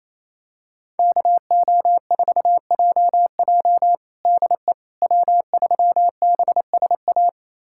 Morse code, heard as a keyed tone, about 28 wpm, 700 Hz. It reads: KO4JJ DE W3BSA
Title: USA Callsign DE Callsign - 5 Characters - 28wpm